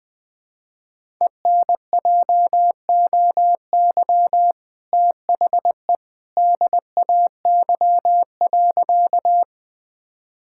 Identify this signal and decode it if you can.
Morse code, heard as a keyed tone, about 21 wpm, 700 Hz. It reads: ENJOY THE DAY.
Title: Sentences from Top 1000 Words - Review Part 5 - 20wpm